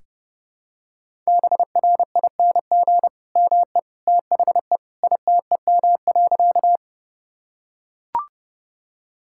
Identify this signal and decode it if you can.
Morse code, heard as a keyed tone, about 30 wpm, 700 Hz. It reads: BRING ME THE ITEM.
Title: Sentences from Top 1000 Words - Review Part 6 - 30wpm